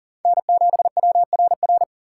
Morse code, heard as a keyed tone, about 38 wpm, 700 Hz. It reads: N7WRR
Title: USA Call Signs - 5 Characters - 3x Character Spacing - 40wpm